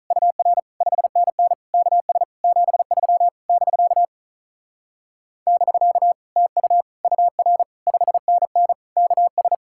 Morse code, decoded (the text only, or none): UR 5NN KS 73 <BK> <BK> TU UR 5NN KS